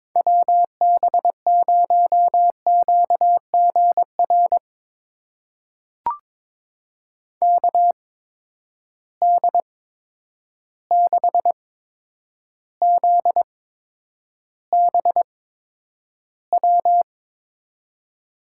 Morse code, heard as a keyed tone, about 23 wpm, spaced as for 10 wpm, 700 Hz. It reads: WB0QGR KD6ZBW
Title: USA Call Signs - 6 Characters - 8x Character Spacing - 22wpm